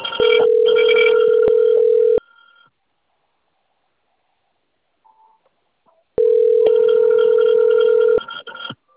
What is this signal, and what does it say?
Ringback tone (North America), 440 + 480 Hz.
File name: rg-1600-unknown- - - .53286